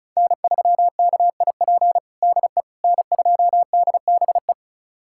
Morse code, heard as a keyed tone, about 36 wpm, 700 Hz. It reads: N3KIP DE N2DBE
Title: USA Callsign DE Callsign - 5 Characters - 35wpm